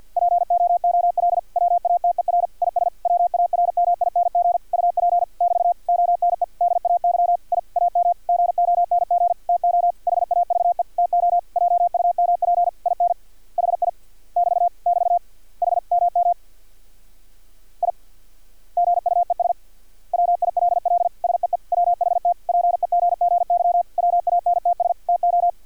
Morse code, heard as a keyed tone, about 50 wpm, 700 Hz. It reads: POOL WATER IS WARMING UP <BT> ONE DAY I AM GONG TO HAVE TO JUMP IN HI <BT> <BT> H MM I GUES WILL SEE WHT PEGGY WANTS TO
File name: archived / CRSnet